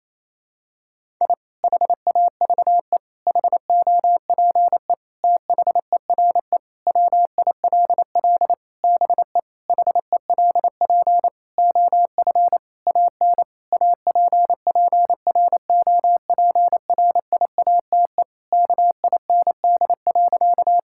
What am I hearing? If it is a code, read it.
Morse code, heard as a keyed tone, about 28 wpm, 700 Hz. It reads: I HAVE HOPE THERE WILL BE HELP OF AN APPROPRIATE KIND.
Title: Sentences from Top 1000 Words - Review Part 5 - 28wpm